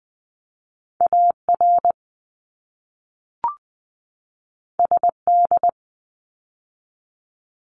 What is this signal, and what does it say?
Morse code, heard as a keyed tone, about 20 wpm, 700 Hz. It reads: AR SD
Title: ARRL Sections 20wpm